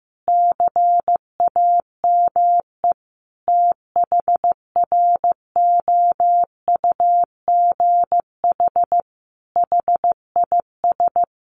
Morse code, heard as a keyed tone, about 14 wpm, 700 Hz. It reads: CAME THROUGH HIS
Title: Sentences from Top 1200 Words 15wpm